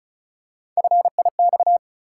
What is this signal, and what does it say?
Morse code, heard as a keyed tone, about 34 wpm, 700 Hz. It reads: FIX
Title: Three Letter Words - 35wpm